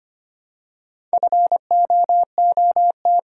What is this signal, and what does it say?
Morse code, heard as a keyed tone, about 25 wpm, 700 Hz. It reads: FOOT
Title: Four Letter Words - 25wpm